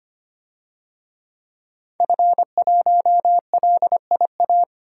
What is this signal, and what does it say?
Morse code, heard as a keyed tone, about 25 wpm, 700 Hz. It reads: F1LIA
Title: DXCC-100 Call Signs - 5 Characters - 25wpm